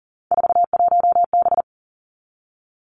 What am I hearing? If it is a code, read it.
Morse code, heard as a keyed tone, about 41 wpm, 700 Hz. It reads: 41B
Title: Three Challenging Letter-Number Combinations 40wpm